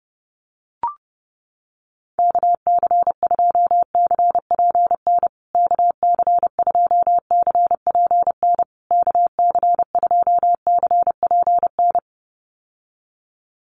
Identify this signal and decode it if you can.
Morse code, heard as a keyed tone, about 30 wpm, 700 Hz. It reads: KC2CPN KC2CPN KC2CPN
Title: USA Call Signs Repeated 3x 30wpm